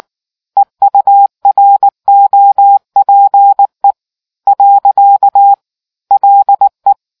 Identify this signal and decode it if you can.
Morse code, heard as a keyed tone, about 19 wpm, 800 Hz. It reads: EUROPE . LE